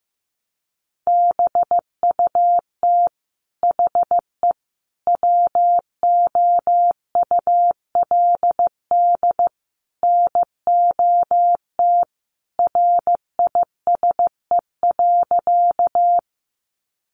Morse code, heard as a keyed tone, about 15 wpm, 700 Hz. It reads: BUT HE WOULD NOT RISE.
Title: Sentences from Top 900 Words 15wpm